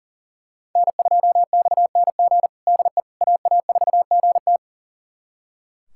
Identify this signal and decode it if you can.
Morse code, heard as a keyed tone, about 38 wpm, 700 Hz. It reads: N2XNG DE AA4GT
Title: USA Callsign DE Callsign - 5 Characters - 40wpm